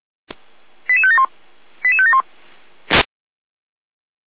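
Five-tone selective call (ZVEI), heard as quick runs of five tones, tones of about 70 ms. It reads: 88571 88571
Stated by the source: ZVEI five-tone selective call received on SDR.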